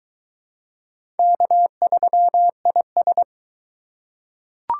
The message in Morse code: K3IS

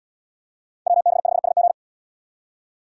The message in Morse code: WB5SG

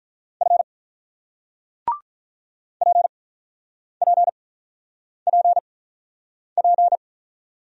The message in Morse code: F PPPP

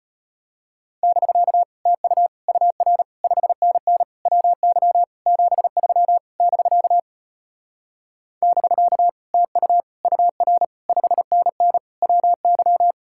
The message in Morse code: <BK> TU UR 5NN WY 73 <BK> <BK> TU UR 5NN WY

